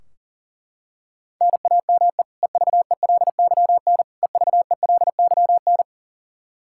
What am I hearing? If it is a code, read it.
Morse code, heard as a keyed tone, about 40 wpm, 700 Hz. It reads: NAME EVELYN EVELYN
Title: Practice QSO - Exchange 1 - Beginning - 40wpm